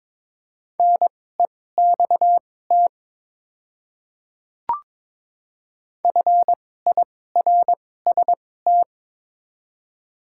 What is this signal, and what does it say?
Morse code, heard as a keyed tone, about 22 wpm, spaced as for 16 wpm, 700 Hz. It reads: NEXT FIRST
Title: 2x Character Spacing - Top 300 Words - 22wpm